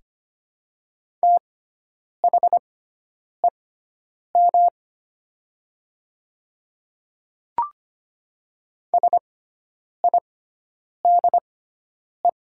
Morse code, heard as a keyed tone, about 25 wpm, spaced as for 9 wpm, 700 Hz. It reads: THEM SIDE
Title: 6x Character Spacing - Top 100 Words - 25wpm